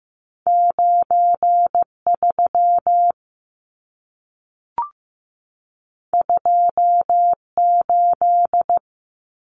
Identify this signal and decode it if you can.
Morse code, heard as a keyed tone, about 15 wpm, 700 Hz. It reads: 93 28
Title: Numbers - 2-Digit 15wpm